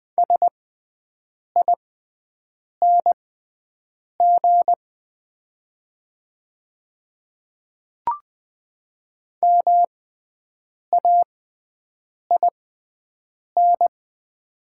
Morse code, heard as a keyed tone, about 20 wpm, spaced as for 7 wpm, 700 Hz. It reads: SING MAIN
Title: 6x Character Spacing - Top 500 Words - 20wpm